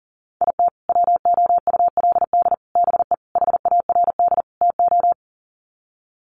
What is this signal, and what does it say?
Morse code, heard as a keyed tone, about 37 wpm, 700 Hz. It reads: IT WOULD BE HARD TO
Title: Sets of 5 Words Encore2 40wpm